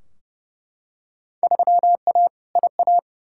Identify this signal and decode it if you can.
Morse code, heard as a keyed tone, about 30 wpm, 700 Hz. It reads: 3A IA
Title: Field Day Exchanges Encore - 30wpm